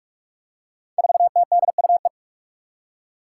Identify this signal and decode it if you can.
Morse code, heard as a keyed tone, about 44 wpm, 700 Hz. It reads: 4TDUE
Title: Five Challenging Letter-Number Combinations 45wpm